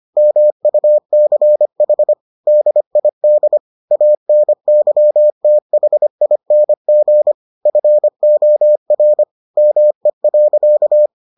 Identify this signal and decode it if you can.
Morse code, heard as a keyed tone, about 25 wpm, 600 Hz. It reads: MUCH DID ANYTHING FOR ME.